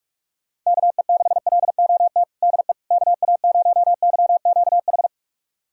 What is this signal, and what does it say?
Morse code, heard as a keyed tone, about 42 wpm, 700 Hz. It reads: KE6LOT DE KA0YQH
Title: USA Callsign DE Callsign 45wpm